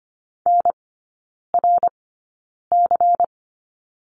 Morse code, heard as a keyed tone, about 25 wpm, spaced as for 9 wpm, 700 Hz. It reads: NRC
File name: course / 25:10wpm / 093 Lesson 16e - Letter c - Three Characters 25wpm